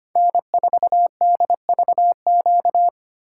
Morse code, heard as a keyed tone, about 33 wpm, 700 Hz. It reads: N4DVQ